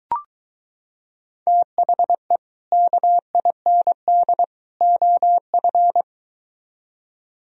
Morse code, heard as a keyed tone, about 23 wpm, 700 Hz. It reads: THE KIND OF